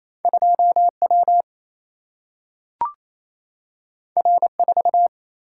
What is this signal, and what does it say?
Morse code, heard as a keyed tone, about 28 wpm, 700 Hz. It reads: 2 W R 4
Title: Single-Letter Single-Number 28wpm